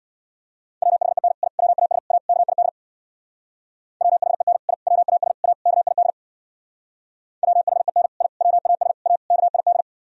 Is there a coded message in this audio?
Yes: Morse code from 0.8 to 9.8 s, heard as a keyed tone, about 64 wpm, 700 Hz. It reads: WHEN I WAS A KID WHEN I WAS A KID WHEN I WAS A KID